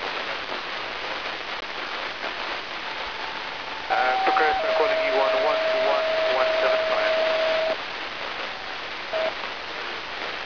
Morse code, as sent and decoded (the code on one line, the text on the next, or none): none